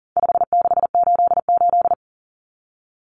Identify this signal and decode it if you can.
Morse code, heard as a keyed tone, about 40 wpm, 700 Hz. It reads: 5688
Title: Common Cut Numbers - 4-Digit 40wpm